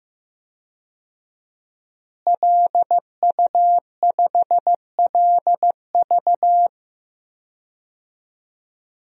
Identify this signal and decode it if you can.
Morse code, heard as a keyed tone, about 15 wpm, 700 Hz. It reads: LU5LV
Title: DXCC-100 Call Signs - 5 Characters - Repeated 3x - 15wpm